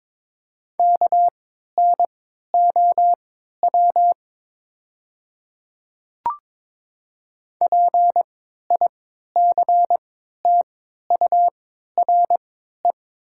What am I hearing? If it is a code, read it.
Morse code, heard as a keyed tone, about 22 wpm, spaced as for 12 wpm, 700 Hz. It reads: KNOW PICTURE
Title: 3x Character Spacing - Top 200 Words - 22wpm